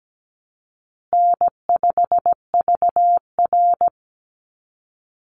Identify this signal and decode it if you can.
Morse code, heard as a keyed tone, about 17 wpm, 700 Hz. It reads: N5VR
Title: Easy Call Signs Repeated 3x - 17wpm